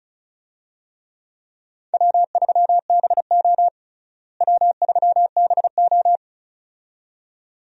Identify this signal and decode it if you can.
Morse code, heard as a keyed tone, about 36 wpm, 700 Hz. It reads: W3BO W3BO
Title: USA Call Signs - 4 Characters - Repeated 2x - 35wpm